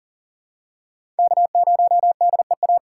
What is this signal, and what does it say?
Morse code, heard as a keyed tone, about 40 wpm, 700 Hz. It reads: K0DEA
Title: USA Call Signs - 5 Characters - 6x Character Spacing - 40wpm